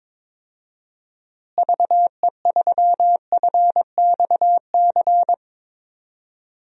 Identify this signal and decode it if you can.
Morse code, heard as a keyed tone, about 22 wpm, 700 Hz. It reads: VE3FXC
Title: DXCC-100 Call Signs - 6 Characters - Repeated 3x - 22wpm